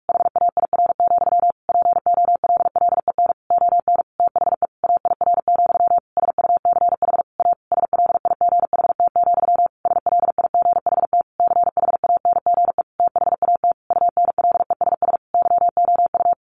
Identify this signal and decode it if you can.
Morse code, heard as a keyed tone, about 44 wpm, 700 Hz. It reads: HAIR, POLLEN ON THE AIR, SUCH A SLIGHT, SLIGHT CHANGE THAT UNLESS YOU
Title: A Sound Of Thunder 45wpm